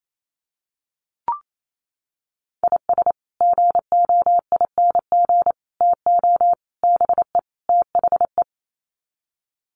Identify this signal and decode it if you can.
Morse code, heard as a keyed tone, about 28 wpm, 700 Hz. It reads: IS GOING TO BE THE
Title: Sets of 5 Words 28wpm